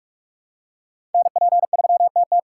Morse code, heard as a keyed tone, about 42 wpm, 700 Hz. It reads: NP3TT
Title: USA Call Signs - 4x Character Spacing - 45wpm